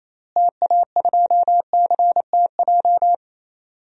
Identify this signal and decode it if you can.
Morse code, heard as a keyed tone, about 27 wpm, 700 Hz. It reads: TA2CTJ
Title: DXCC-100 Call Signs - 6 Characters - Repeated 2x - 28wpm